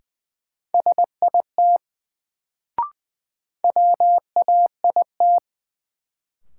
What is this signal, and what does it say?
Morse code, heard as a keyed tone, about 20 wpm, 700 Hz. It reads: SIT WAIT
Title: Sentences from Top 500 Words - New Word Drill - 20wpm